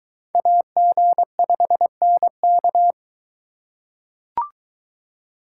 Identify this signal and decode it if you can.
Morse code, heard as a keyed tone, about 23 wpm, 700 Hz. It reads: AG5NK